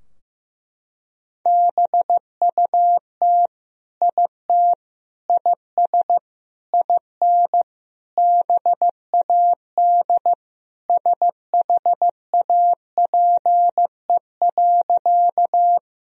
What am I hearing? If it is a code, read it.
Morse code, heard as a keyed tone, about 15 wpm, 700 Hz. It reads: BUT IT IS IN BAD SHAPE.
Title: Sentences from Top 1000 Words - Review Part 3 - 15wpm